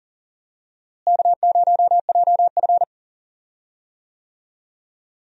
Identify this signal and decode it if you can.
Morse code, heard as a keyed tone, about 40 wpm, 700 Hz. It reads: K0JF
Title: USA Call Signs - 4 Characters - Repeated 2x - 40wpm